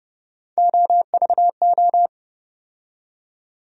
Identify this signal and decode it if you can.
Morse code, heard as a keyed tone, about 30 wpm, 700 Hz. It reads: OVO
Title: USA Call Sign Postfixes - 3 Letters 30wpm